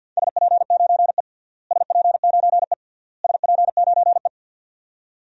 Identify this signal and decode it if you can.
Morse code, heard as a keyed tone, about 46 wpm, 700 Hz. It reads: SP9E SP9E SP9E
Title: DXCC-100 Call Signs - 4 Characters - Repeated 3x - 50wpm